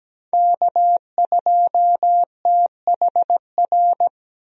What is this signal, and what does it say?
Morse code, heard as a keyed tone, about 17 wpm, 700 Hz. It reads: K2THR
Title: USA Call Signs - 5 Characters - 17wpm